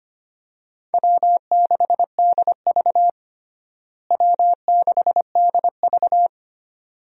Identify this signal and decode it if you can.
Morse code, heard as a keyed tone, about 25 wpm, 700 Hz. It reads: W6DV W6DV
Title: USA Call Signs - 4 Characters - Repeated 2x - 25wpm